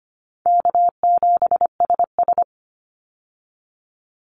Morse code, heard as a keyed tone, about 25 wpm, 700 Hz. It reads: K7SS
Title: Call Signs Repeated 2x - 25wpm